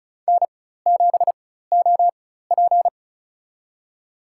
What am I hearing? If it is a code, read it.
Morse code, heard as a keyed tone, about 36 wpm, spaced as for 16 wpm, 700 Hz. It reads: N7OP
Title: USA Call Signs - 4 Characters - 4x Character Spacing - 35wpm